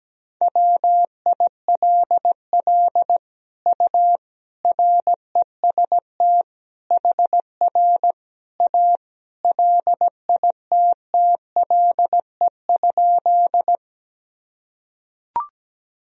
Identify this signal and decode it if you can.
Morse code, heard as a keyed tone, about 17 wpm, 700 Hz. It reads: WILL U REST HR A LITTLE?
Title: Sentences from Top 500 Words and Abbreviations 17wpm